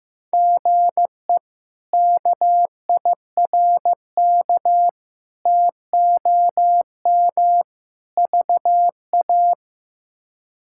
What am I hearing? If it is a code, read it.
Morse code, heard as a keyed tone, about 16 wpm, 700 Hz. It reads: GE KIRK TOM VA
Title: K1USN SST QSO Practice - 15wpm